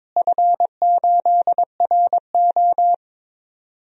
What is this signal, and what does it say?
Morse code, heard as a keyed tone, about 22 wpm, 700 Hz. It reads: F8RO
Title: DXCC-100 Call Signs - 4 Characters - Repeated 3x - 22wpm